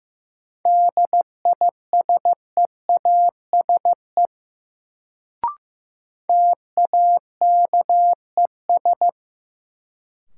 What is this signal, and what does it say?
Morse code, heard as a keyed tone, about 15 wpm, 700 Hz. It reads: DISEASE TAKES
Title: Sentences from Top 1000 Words - Review Part 6 - New Word Drill - 15wpm